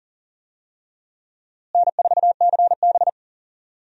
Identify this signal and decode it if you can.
Morse code, heard as a keyed tone, about 40 wpm, 700 Hz. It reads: N4CB
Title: USA Call Signs - 4 Characters - Repeated 3x - 40wpm